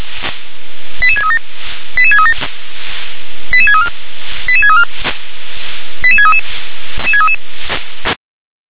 Five-tone selective call (ZVEI), heard as quick runs of five tones, tones of about 70 ms. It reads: 88637 88637 88634 88634 88633